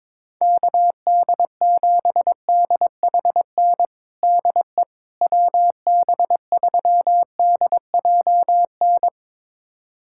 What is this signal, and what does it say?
Morse code, heard as a keyed tone, about 22 wpm, 700 Hz. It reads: KD7DHN DE WB3DJN
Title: USA Callsign DE Callsign 22wpm